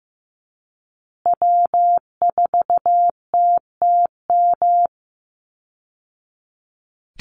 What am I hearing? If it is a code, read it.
Morse code, heard as a keyed tone, about 15 wpm, 700 Hz. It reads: W4TTM
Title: Call Signs Repeated 3x - 15wpm